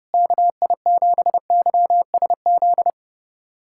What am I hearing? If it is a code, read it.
Morse code, heard as a keyed tone, about 31 wpm, 700 Hz. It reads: KI7YSZ
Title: USA Callsign DE Callsign - 6 Characters - 30wpm